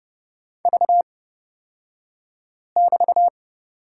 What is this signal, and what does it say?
Morse code, heard as a keyed tone, about 30 wpm, spaced as for 5 wpm, 700 Hz. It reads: V<BT>
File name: Lesson 36d - BT - Two Characters 30wpm